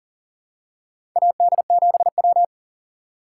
Morse code, heard as a keyed tone, about 40 wpm, 700 Hz. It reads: AD7W
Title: USA Call Signs - 2x Character Spacing - 40wpm